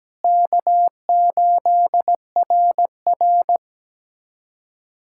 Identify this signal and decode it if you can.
Morse code, heard as a keyed tone, about 17 wpm, 700 Hz. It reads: K8RR
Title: K1USN SST QSO Practice - 17wpm